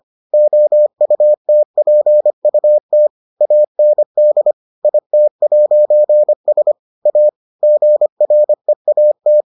Morse code, heard as a keyed tone, about 25 wpm, 600 Hz. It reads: OUTPUT AND IT'S A GREAT